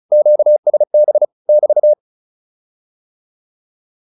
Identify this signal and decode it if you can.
Morse code, heard as a keyed tone, about 34 wpm, 600 Hz. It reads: QSB <BT>
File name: QSO105 35